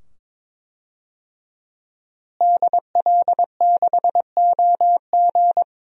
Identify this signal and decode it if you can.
Morse code, heard as a keyed tone, about 22 wpm, 700 Hz. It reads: DL6OG
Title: DXCC-100 Call Signs - 5 Characters - Repeated 2x - 22wpm